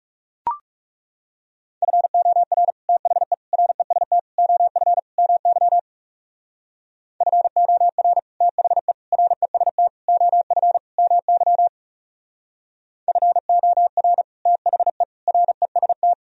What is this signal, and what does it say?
Morse code, heard as a keyed tone, about 38 wpm, 700 Hz. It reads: FOR THE REST OF MY FOR THE REST OF MY FOR THE REST